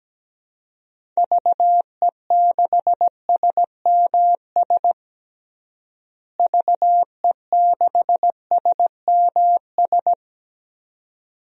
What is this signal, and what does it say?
Morse code, heard as a keyed tone, about 17 wpm, 700 Hz. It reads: VE6SMS VE6SMS
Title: DXCC-100 Call Signs - 6 Characters - Repeated 2x - 17wpm